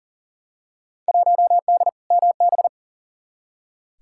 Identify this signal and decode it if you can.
Morse code, heard as a keyed tone, about 40 wpm, 700 Hz. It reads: 1D MB
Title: Field Day Exchanges Encore - 40wpm